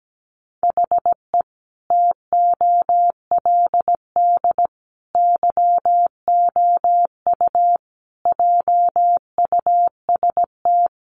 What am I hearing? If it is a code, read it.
Morse code, heard as a keyed tone, about 17 wpm, 700 Hz. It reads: HE TOLD YOU JUST
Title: Sentences from Top 900 Words 17wpm